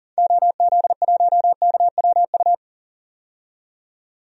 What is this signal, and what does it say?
Morse code, heard as a keyed tone, about 40 wpm, 700 Hz. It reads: OZ1KWU
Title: DXCC-100 Call Signs - 6 Characters - 40wpm